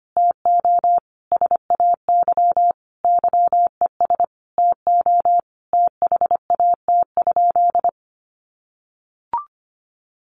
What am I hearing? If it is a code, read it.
Morse code, heard as a keyed tone, about 25 wpm, 700 Hz. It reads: TO SAY YES TO THAT?
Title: Sentences from Top 1100 Words 25wpm